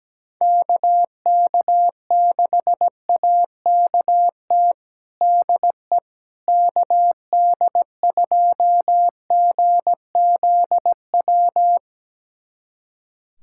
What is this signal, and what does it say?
Morse code, heard as a keyed tone, about 17 wpm, 700 Hz. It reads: KK6AKT DE KD2GZW
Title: USA Callsign DE Callsign - 6 Characters - 17wpm